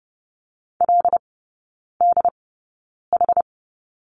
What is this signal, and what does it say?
Morse code, heard as a keyed tone, about 30 wpm, spaced as for 9 wpm, 700 Hz. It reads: LDH
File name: Lesson 17e - Letter u - Three Characters 30wpm